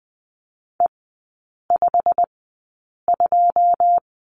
Morse code, heard as a keyed tone, about 20 wpm, spaced as for 8 wpm, 700 Hz. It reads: E52
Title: 093 Lesson 16e - Letter c - Three Characters 20wpm